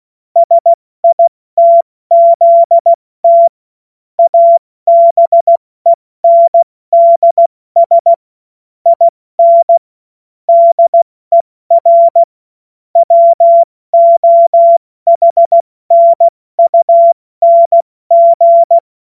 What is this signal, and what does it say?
Morse code, heard as a keyed tone, about 16 wpm, 680 Hz. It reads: SITZT ABENDS IN DER WOHNUNG
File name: Morsekurs / Witze / Witz 16. November 2021, 70 F80